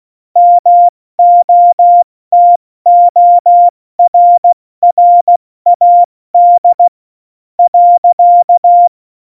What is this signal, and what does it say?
Morse code, heard as a keyed tone, about 16 wpm, spaced as for 14 wpm, 700 Hz. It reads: MOTORRAD .
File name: Morsekurs / Witze / Witz 22. Mai 2018, 70 F80